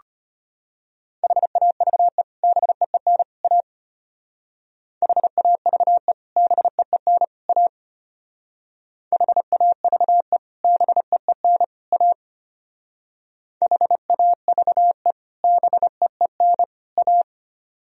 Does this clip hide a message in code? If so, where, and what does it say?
Yes: Morse code from 1.2 to 17.2 s, heard as a keyed tone, about 32 wpm, 700 Hz. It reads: HAVE BEEN A HAVE BEEN A HAVE BEEN A HAVE BEEN A